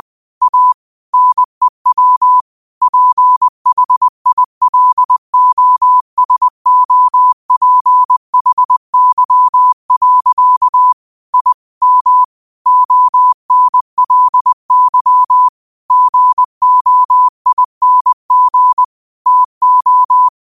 Morse code, heard as a keyed tone, about 20 wpm, 1000 Hz. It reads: A NEW PHILOSOPHY. I M ONLY GOING TO